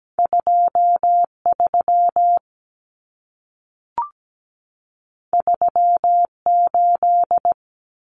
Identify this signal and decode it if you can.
Morse code, heard as a keyed tone, about 17 wpm, 700 Hz. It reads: 23 38
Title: Numbers - 2-Digit 17wpm